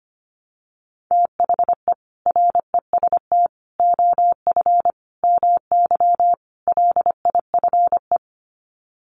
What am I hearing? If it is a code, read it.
Morse code, heard as a keyed tone, about 25 wpm, 700 Hz. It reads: THE REST OF MY LIFE
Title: Sets of 5 Words 25wpm